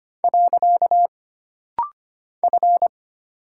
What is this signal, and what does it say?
Morse code, heard as a keyed tone, about 25 wpm, 700 Hz. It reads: . F
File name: Single Character (Letter:Number:Punctuation:Prosign) - Rapid-Fire - 25wpm